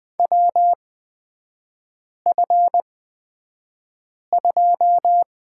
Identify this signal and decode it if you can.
Morse code, heard as a keyed tone, about 20 wpm, spaced as for 5 wpm, 700 Hz. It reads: WF2
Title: 236 Lesson 38e - Letter q - Three Characters 20wpm